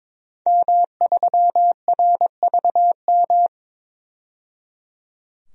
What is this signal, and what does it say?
Morse code, heard as a keyed tone, about 22 wpm, 700 Hz. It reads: M3RVM
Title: DXCC-100 Call Signs - 5 Characters - 22wpm